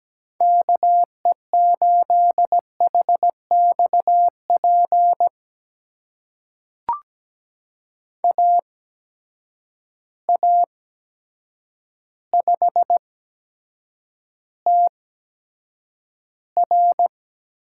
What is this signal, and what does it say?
Morse code, heard as a keyed tone, about 17 wpm, spaced as for 8 wpm, 700 Hz. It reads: KE8HXP AA5TR
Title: USA Call Signs - 8x Character Spacing - 17wpm